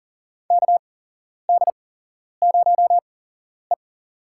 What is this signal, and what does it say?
Morse code, heard as a keyed tone, about 37 wpm, spaced as for 11 wpm, 700 Hz. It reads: KD0E
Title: USA Call Signs - 4 Characters - 8x Character Spacing - 40wpm